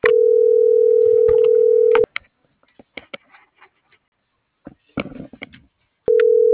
Ringback tone (North America), 440 + 480 Hz.
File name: CENTRAL MAERSK / out sandovalmsk - .70488